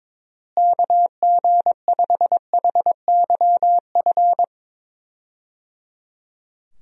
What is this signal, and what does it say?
Morse code, heard as a keyed tone, about 22 wpm, 700 Hz. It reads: KG5HYF
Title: USA Call Signs - 6 Characters - Repeated 3x - 22wpm